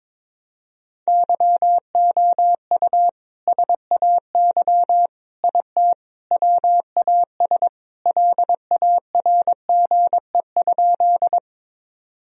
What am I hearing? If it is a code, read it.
Morse code, heard as a keyed tone, about 22 wpm, 700 Hz. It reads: YOU SAY IT WAS LARGE?
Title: Sentences from Top 1000 Words - Review Part 1 - 22wpm